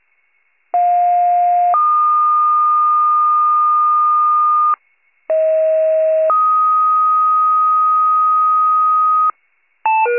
Two-tone page, heard as a one-second tone, then a three-second tone, about 710 then 1220 Hz, about 640 then 1250 Hz. It is heard from 0.7 to 9.3 s.
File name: pager / 38 FIRE / wcalert